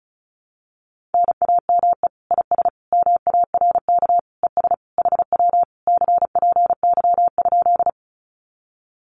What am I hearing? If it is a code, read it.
Morse code, heard as a keyed tone, about 36 wpm, 700 Hz. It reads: NAME IS MARK ES HW CPY?
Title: Common QSO Phrases Encore 35wpm